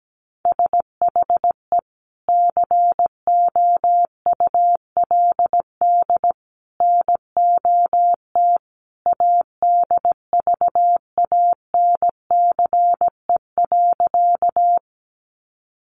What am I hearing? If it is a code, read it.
Morse code, heard as a keyed tone, about 17 wpm, 700 Hz. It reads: SHE COULD NOT ADVANCE.
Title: Sentences from Top 1400 Words 17wpm